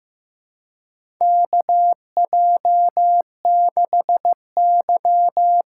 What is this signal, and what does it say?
Morse code, heard as a keyed tone, about 15 wpm, 700 Hz. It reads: KJ6Y
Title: USA Call Signs - 6 Characters - 8x Character Spacing - 15wpm